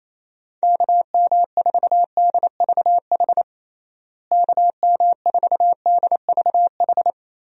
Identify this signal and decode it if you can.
Morse code, heard as a keyed tone, about 28 wpm, 700 Hz. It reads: KM4DVH KM4DVH
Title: USA Call Signs - 6 Characters - Repeated 3x - 28wpm